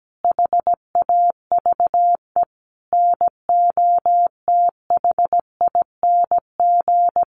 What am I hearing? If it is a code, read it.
Morse code, heard as a keyed tone, about 17 wpm, 700 Hz. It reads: HAVE NOTHING
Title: Sentences from Top 1100 Words 17wpm